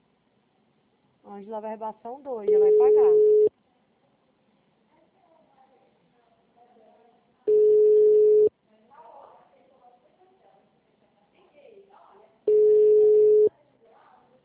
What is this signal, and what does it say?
Ringback tone (Europe, 425 Hz), with speech.